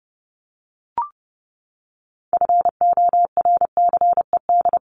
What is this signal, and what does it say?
Morse code, heard as a keyed tone, about 29 wpm, 700 Hz. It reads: FORCED
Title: ED Suffix - Top 100 Words with Base Words 30wpm